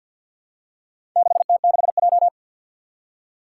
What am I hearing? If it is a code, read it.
Morse code, heard as a keyed tone, about 51 wpm, 700 Hz. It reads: 6T6J
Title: Four Challenging Letter-Number Combinations 50wpm